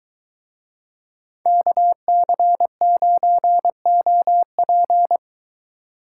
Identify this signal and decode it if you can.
Morse code, heard as a keyed tone, about 23 wpm, 700 Hz. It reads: KC9OP